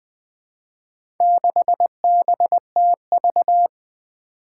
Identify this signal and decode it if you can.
Morse code, heard as a keyed tone, about 20 wpm, 700 Hz. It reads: 6BTV
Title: Four Challenging Letter-Number Combinations 20wpm